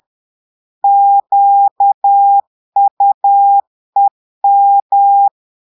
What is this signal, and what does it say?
Morse code, heard as a keyed tone, about 10 wpm, 800 Hz. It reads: QUEM
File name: portugais texte en clair 2 vitesse 10 mots minutes